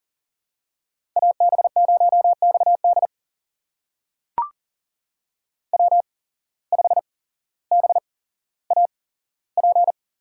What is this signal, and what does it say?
Morse code, heard as a keyed tone, about 38 wpm, spaced as for 17 wpm, 700 Hz. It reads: AB0XD W5BAP
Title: USA Call Signs - 5 Characters - 8x Character Spacing - 40wpm